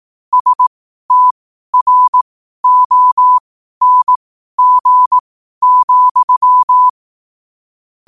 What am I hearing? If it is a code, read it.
Morse code, heard as a keyed tone, about 18 wpm, spaced as for 12 wpm, 1000 Hz. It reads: STRONG,
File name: qotdfiles / 12wpm